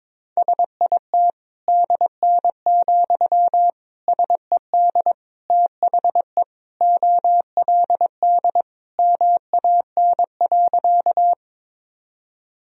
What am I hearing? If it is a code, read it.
Morse code, heard as a keyed tone, about 22 wpm, 700 Hz. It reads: SIT DN, SED THE OLD MAN.
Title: Sentences from Top 500 Words and Abbreviations 22wpm